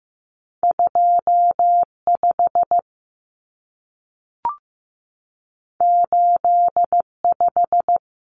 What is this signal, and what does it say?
Morse code, heard as a keyed tone, about 15 wpm, 700 Hz. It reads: 2 5 8 5
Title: Numbers - 2-Digit 15wpm